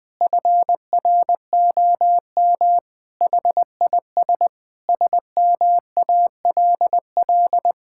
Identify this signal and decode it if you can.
Morse code, heard as a keyed tone, about 21 wpm, 700 Hz. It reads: FROM HIS SMALL
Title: Sentences from Top 1600 Words 20wpm